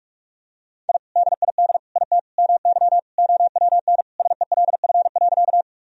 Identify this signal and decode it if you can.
Morse code, heard as a keyed tone, about 41 wpm, 700 Hz. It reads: I DID IT MY OWN SELF.
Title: Sentences from Top 1000 Words - Review Part 1 - 45wpm